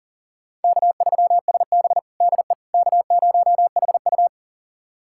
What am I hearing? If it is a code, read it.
Morse code, heard as a keyed tone, about 40 wpm, 700 Hz. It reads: K3SB DE K0HU
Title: USA Callsign DE Callsign - 4 Characters - 40wpm